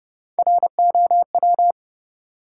Morse code, heard as a keyed tone, about 31 wpm, 700 Hz. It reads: ROW
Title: Three Letter Words - 30wpm